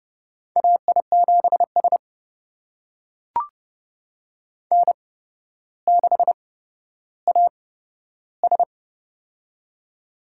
Morse code, heard as a keyed tone, about 30 wpm, spaced as for 13 wpm, 700 Hz. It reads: AI7S N6AS
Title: USA Call Signs - 4 Characters - 8x Character Spacing - 30wpm